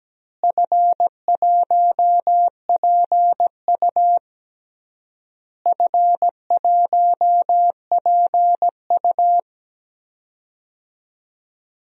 Morse code, heard as a keyed tone, about 17 wpm, 700 Hz. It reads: F1PU F1PU
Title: DXCC-100 Call Signs - 4 Characters - Repeated 3x - 17wpm